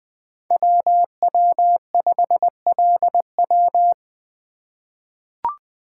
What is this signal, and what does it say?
Morse code, heard as a keyed tone, about 20 wpm, 700 Hz. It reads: WW5LW
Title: USA Call Signs - 2x Character Spacing - 20wpm